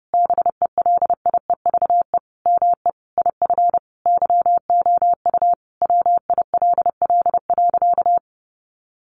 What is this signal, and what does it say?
Morse code, heard as a keyed tone, about 30 wpm, 700 Hz. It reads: BELIEVE ME IF YOU WILL.
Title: Sentences from Top 1000 Words 30wpm